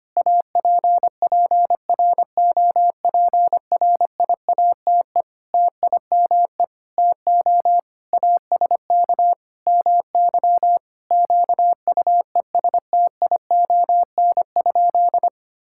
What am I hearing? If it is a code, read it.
Morse code, heard as a keyed tone, about 25 wpm, 700 Hz. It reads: APPROPRIATE TIME TO ASK MY QUESTION?
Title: Sentences from Top 1000 Words - Review Part 5 - 25wpm